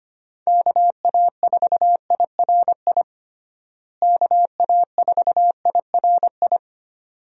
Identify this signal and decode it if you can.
Morse code, heard as a keyed tone, about 25 wpm, 700 Hz. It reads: KA4IRI KA4IRI
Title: USA Call Signs - 6 Characters - Repeated 2x - 25wpm